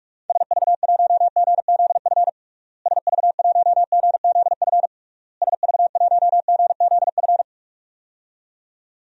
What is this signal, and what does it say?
Morse code, heard as a keyed tone, about 46 wpm, 700 Hz. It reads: SV1GZF SV1GZF SV1GZF
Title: DXCC-100 Call Signs - 6 Characters - Repeated 3x - 45wpm